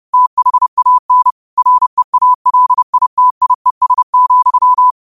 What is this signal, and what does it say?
Morse code, heard as a keyed tone, about 30 wpm, 1000 Hz. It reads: THAN REALITIES,
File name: qotdfiles / 30wpm